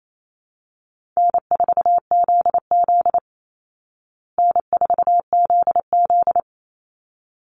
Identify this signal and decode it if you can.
Morse code, heard as a keyed tone, about 28 wpm, 700 Hz. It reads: N4ZZ N4ZZ
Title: Call Signs Repeated 3x - 28wpm